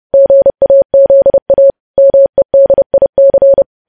Morse code, heard as a keyed tone, about 30 wpm, 550 Hz. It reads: GAZA MEDIC
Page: news headlines in morse code at 30 wpm